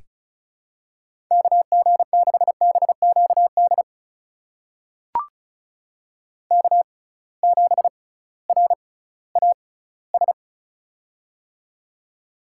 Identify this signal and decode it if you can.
Morse code, heard as a keyed tone, about 34 wpm, spaced as for 19 wpm, 700 Hz. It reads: KG6BQD K7RAS
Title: USA Call Signs - 6x Character Spacing - 35wpm